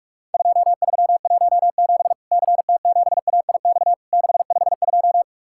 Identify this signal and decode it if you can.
Morse code, heard as a keyed tone, about 44 wpm, 700 Hz. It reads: 2317 CT7AIX 652
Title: Worked All Europe DX Contest - QTC Exchange - 45wpm